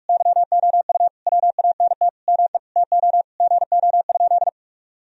Morse code, heard as a keyed tone, about 43 wpm, 700 Hz. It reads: YOU WANT ME TO GO?
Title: Sentences from Top 1000 Words - Review Part 3 - 45wpm